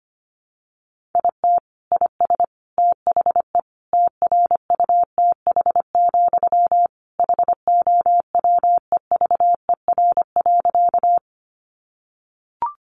Morse code, heard as a keyed tone, about 25 wpm, 700 Hz. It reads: IT IS THE TRUTH, HOWEVER.
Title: Sentences from Top 800 Words 25wpm